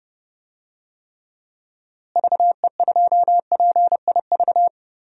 Morse code, heard as a keyed tone, about 30 wpm, 700 Hz. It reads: VE2PIV
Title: DXCC-100 Call Signs - 6 Characters - Repeated 3x - 30wpm